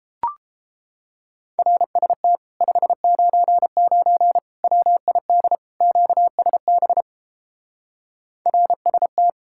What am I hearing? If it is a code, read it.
Morse code, heard as a keyed tone, about 29 wpm, 700 Hz. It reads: RST 599 WID QSB RST